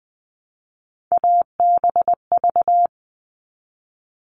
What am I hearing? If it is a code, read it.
Morse code, heard as a keyed tone, about 19 wpm, 700 Hz. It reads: ABV
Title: All Cut Numbers - 3-Digit 20wpm